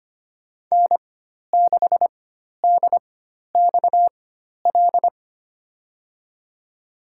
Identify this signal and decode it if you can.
Morse code, heard as a keyed tone, about 25 wpm, spaced as for 12 wpm, 700 Hz. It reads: N6DXL
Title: USA Call Signs - 5 Characters - 4x Character Spacing - 25wpm